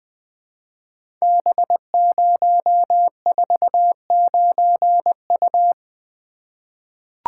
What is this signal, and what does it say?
Morse code, heard as a keyed tone, about 21 wpm, 700 Hz. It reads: B049U
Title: Code Groups - 5 Letter:Numbers - 20wpm